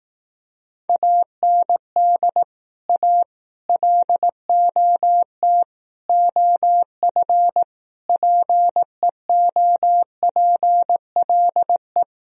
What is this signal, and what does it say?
Morse code, heard as a keyed tone, about 18 wpm, 700 Hz. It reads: AND A LOT OF PEOPLE
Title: Speed-Racing - Sets of 5 Words 15wpm